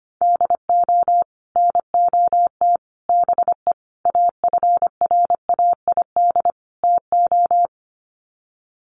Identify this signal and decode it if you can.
Morse code, heard as a keyed tone, about 25 wpm, 700 Hz. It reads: DO NOT BE AFRAID TO
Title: Sets of 5 Words Encore2 25wpm